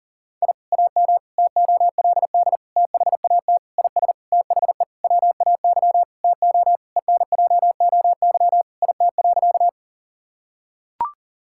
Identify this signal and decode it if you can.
Morse code, heard as a keyed tone, about 40 wpm, 700 Hz. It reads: I AM TOLD THAT IS THE WAY TO ENJOY IT.
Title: Sentences from Top 1000 Words - Review Part 5 - 40wpm